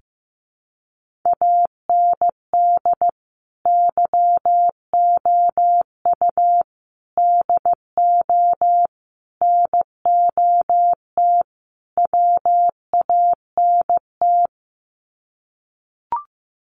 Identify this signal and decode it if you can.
Morse code, heard as a keyed tone, about 14 wpm, 700 Hz. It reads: AND YOU DO NOT WANT
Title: Sets of 5 Words Encore2 15wpm